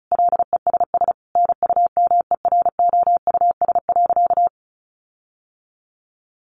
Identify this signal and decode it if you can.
Morse code, heard as a keyed tone, about 34 wpm, 700 Hz. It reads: LESS NUMEROUS.
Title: Sentences from Top 1400 Words 35wpm